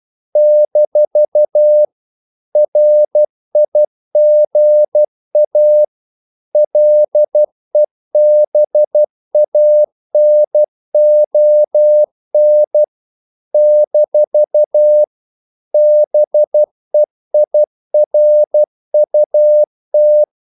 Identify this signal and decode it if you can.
Morse code, heard as a keyed tone, about 12 wpm, 600 Hz. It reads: - RIGA LEBANON - BEIRUT
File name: CW / Practice / NationalCapitals12